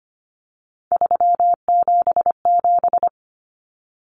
Morse code, heard as a keyed tone, about 25 wpm, 700 Hz. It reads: 377
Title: Common Cut Numbers - 3-Digit 25wpm